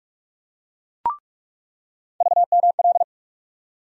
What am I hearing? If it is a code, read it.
Morse code, heard as a keyed tone, about 45 wpm, 700 Hz. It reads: VML
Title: USA Call Sign Postfixes - 3 Letters 45wpm